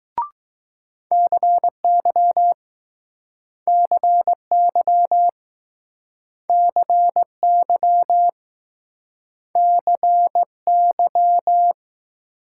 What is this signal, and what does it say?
Morse code, heard as a keyed tone, about 20 wpm, 700 Hz. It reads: CY CY CY CY